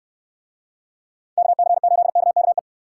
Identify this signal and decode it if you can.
Morse code, heard as a keyed tone, about 68 wpm, 700 Hz. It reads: KC8KKE